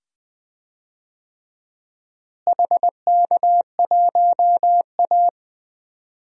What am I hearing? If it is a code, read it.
Morse code, heard as a keyed tone, about 20 wpm, 700 Hz. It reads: HK1A
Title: DXCC-100 Call Signs - 4 Characters - Repeated 2x - 20wpm